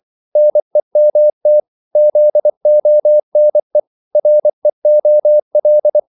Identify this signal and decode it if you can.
Morse code, heard as a keyed tone, about 24 wpm, 600 Hz. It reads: NEMT ZONE REOL
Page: Korte danske ord 24wpm